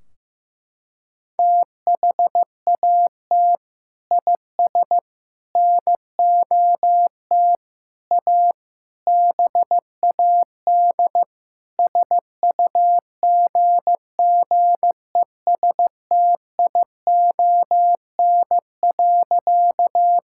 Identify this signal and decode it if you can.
Morse code, heard as a keyed tone, about 16 wpm, 700 Hz. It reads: THAT IS NOT A BAD SUGGESTION.
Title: Sentences from Top 1600 Words 15wpm